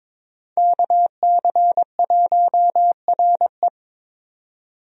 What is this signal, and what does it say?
Morse code, heard as a keyed tone, about 22 wpm, 700 Hz. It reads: KC1RE
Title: USA Call Signs - 5 Characters - 22wpm